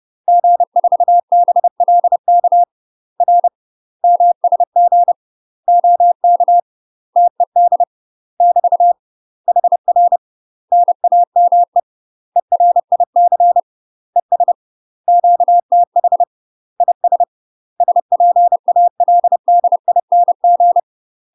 Morse code, heard as a keyed tone, about 30 wpm, 700 Hz. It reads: G4BLK R MSG OK TED <BT> HR NAME ERIC ES QTH IS SPALDING
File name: UK MASTER QSOs TRAINING / 30WPM / UK MASTER QSOs 0015